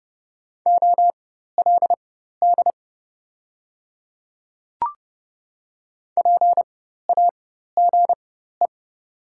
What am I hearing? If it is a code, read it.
Morse code, heard as a keyed tone, about 30 wpm, spaced as for 14 wpm, 700 Hz. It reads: OLD PAGE
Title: 4x Character Spacing - Top 300 Words - 30wpm